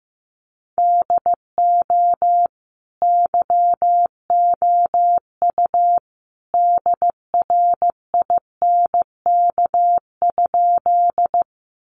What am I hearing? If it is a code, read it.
Morse code, heard as a keyed tone, about 15 wpm, 700 Hz. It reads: DO YOU DRINK?
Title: Sentences from Top 1500 Words 15wpm